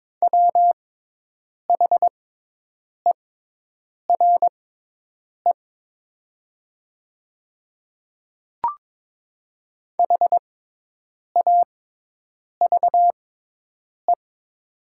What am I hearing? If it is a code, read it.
Morse code, heard as a keyed tone, about 22 wpm, spaced as for 8 wpm, 700 Hz. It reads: WHERE HAVE
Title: 6x Character Spacing - Top 200 Words - 22wpm